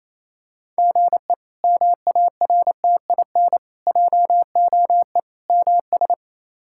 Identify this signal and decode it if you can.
Morse code, heard as a keyed tone, about 28 wpm, 700 Hz. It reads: GE MARTIN JOE MS
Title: K1USN SST QSO Practice - 28wpm